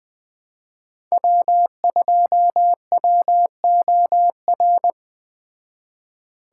Morse code, heard as a keyed tone, about 20 wpm, 700 Hz. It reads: W2WOR
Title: Speed Racing - USA 1x3 Call Signs 17wpm